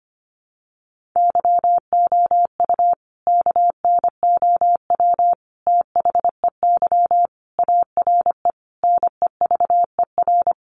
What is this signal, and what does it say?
Morse code, heard as a keyed tone, about 25 wpm, 700 Hz. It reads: YOU KNOW THEY ARE NEVER
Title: Sentences from Top 1400 Words 25wpm